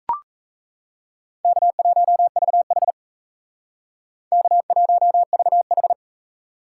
Morse code, heard as a keyed tone, about 35 wpm, 700 Hz. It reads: K1VH K1VH
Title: Speed Racing - USA 1x2 Call Signs 28wpm